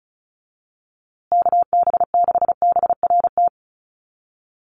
Morse code, heard as a keyed tone, about 34 wpm, 700 Hz. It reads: KB6BRT
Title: USA Call Signs Repeated 3x 35wpm